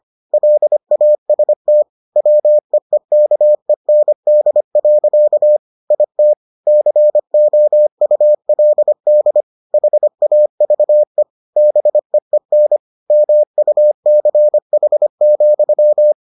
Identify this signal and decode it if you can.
Morse code, heard as a keyed tone, about 25 wpm, 600 Hz. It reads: LAST WEEKEND. IT COULD HAVE BEEN MUCH,